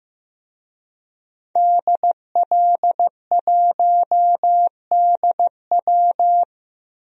Morse code, heard as a keyed tone, about 15 wpm, 700 Hz. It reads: DL1DW
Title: DXCC-100 Call Signs - 5 Characters - 15wpm